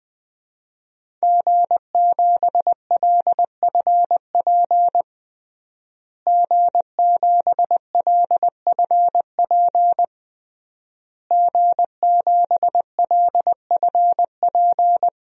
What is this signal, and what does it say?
Morse code, heard as a keyed tone, about 20 wpm, 700 Hz. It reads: G7LFP G7LFP G7LFP
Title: DXCC-100 Call Signs - 5 Characters - Repeated 3x - 20wpm